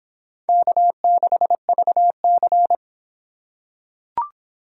Morse code, heard as a keyed tone, about 26 wpm, 700 Hz. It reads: K6VC